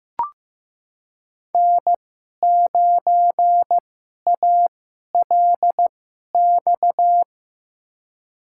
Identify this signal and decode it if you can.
Morse code, heard as a keyed tone, about 15 wpm, spaced as for 11 wpm, 700 Hz. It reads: N9ALX
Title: USA Call Signs - 5 Characters - 2x Character Spacing - 15wpm